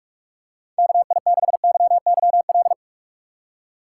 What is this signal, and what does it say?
Morse code, heard as a keyed tone, about 47 wpm, 700 Hz. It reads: KI6YYL